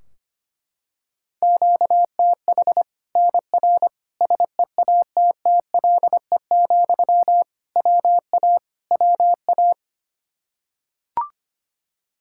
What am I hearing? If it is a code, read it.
Morse code, heard as a keyed tone, about 25 wpm, 700 Hz. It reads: QTH NR SEATTLE, WA WA
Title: Practice QSO - Exchange 1 - Beginning - 25wpm